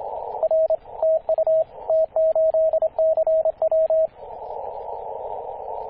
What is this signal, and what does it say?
Morse code, heard as a keyed tone, about 27 wpm, 640 Hz, with radio noise. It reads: R TU T8CW
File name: t8cw-18-0737mhz-cw